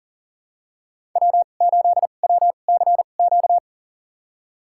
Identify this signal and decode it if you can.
Morse code, heard as a keyed tone, about 40 wpm, spaced as for 29 wpm, 700 Hz. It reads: W8WCQ
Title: USA Call Signs - 5 Characters - 2x Character Spacing - 40wpm